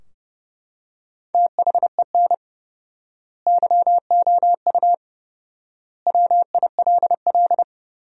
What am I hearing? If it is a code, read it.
Morse code, heard as a keyed tone, about 30 wpm, 700 Hz. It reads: THEN YOU WILL
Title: 4x Word-Spacing - Sentences from Top 400 Words - 30wpm